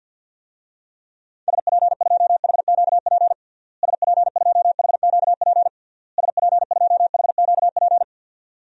Morse code, heard as a keyed tone, about 50 wpm, 700 Hz. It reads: SP2HQP SP2HQP SP2HQP
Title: DXCC-100 Call Signs - 6 Characters - Repeated 3x - 50wpm